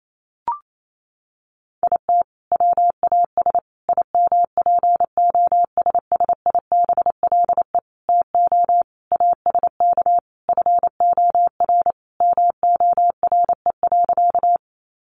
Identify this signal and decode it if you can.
Morse code, heard as a keyed tone, about 28 wpm, 700 Hz. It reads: IT WAS IMPOSSIBLE TO ASK FOR MORE.
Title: Sentences from Top 1100 Words 28wpm